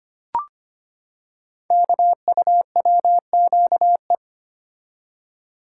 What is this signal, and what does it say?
Morse code, heard as a keyed tone, about 25 wpm, 700 Hz. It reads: KUWQE
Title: Code Groups - 5 Letters - 25wpm